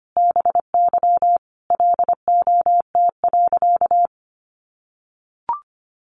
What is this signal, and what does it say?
Morse code, heard as a keyed tone, about 25 wpm, 700 Hz. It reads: BY LOT.
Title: Sentences from Top 1500 Words 25wpm